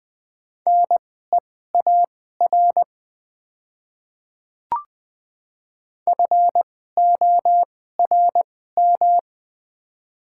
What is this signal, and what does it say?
Morse code, heard as a keyed tone, about 21 wpm, spaced as for 15 wpm, 700 Hz. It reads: NEAR FORM
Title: 2x Character Spacing - Top 200 Words - 20wpm